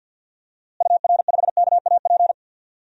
Morse code, heard as a keyed tone, about 47 wpm, 700 Hz. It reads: UR5KAP
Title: DXCC-100 Call Signs - 6 Characters - Repeated 3x - 50wpm